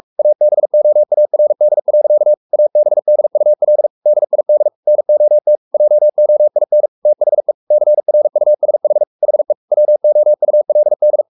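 Morse code, heard as a keyed tone, about 44 wpm, 600 Hz. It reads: ABOARD. ABDUL DID NOT JOIN THE CRUSH HE WOULD
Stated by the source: Morse code at 43 wpm sending Ken Follett's novel Never in English, from CW med Gnister.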